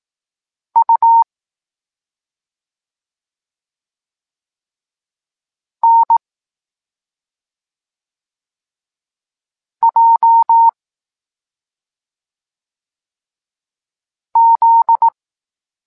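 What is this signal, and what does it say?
Morse code, heard as a keyed tone, about 18 wpm, 920 Hz. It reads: U N J Z
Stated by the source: Evenly spaced Morse lesson text 21 for learning letters G and Z, from SP9MOA Niepołomicki Klub Krótkofalowców.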